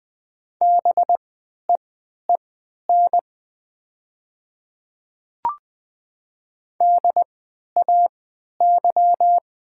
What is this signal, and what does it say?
Morse code, heard as a keyed tone, about 20 wpm, spaced as for 11 wpm, 690 Hz. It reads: BEEN DAY
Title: 3x Character Spacing - Top 100 Words - 20wpm